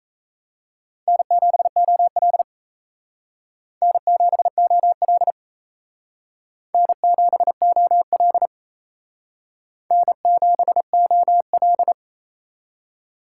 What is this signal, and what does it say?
Morse code, heard as a keyed tone, about 35 wpm, 700 Hz. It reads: N7OL N7OL N7OL N7OL
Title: Speed Racing - USA 1x2 Call Signs 28wpm